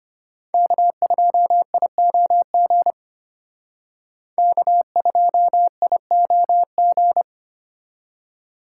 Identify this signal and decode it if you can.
Morse code, heard as a keyed tone, about 31 wpm, 700 Hz. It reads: K2IOG K2IOG
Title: Speed Racing - USA 1x3 Call Signs 25wpm